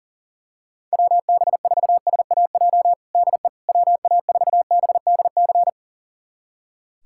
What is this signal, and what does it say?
Morse code, heard as a keyed tone, about 40 wpm, 700 Hz. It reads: WB4SAJ DE WA4BDC
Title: USA Callsign DE Callsign - 6 Characters - 40wpm